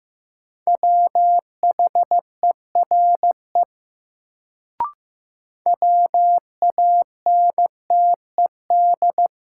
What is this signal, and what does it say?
Morse code, heard as a keyed tone, about 15 wpm, 700 Hz. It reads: WHERE WANTED
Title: Sets of 2 Words - New Word Drill - 15wpm